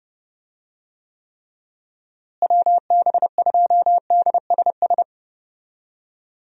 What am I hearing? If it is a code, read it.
Morse code, heard as a keyed tone, about 32 wpm, 700 Hz. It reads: WB2DSS
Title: USA Call Signs - 6 Characters - Repeated 2x - 30wpm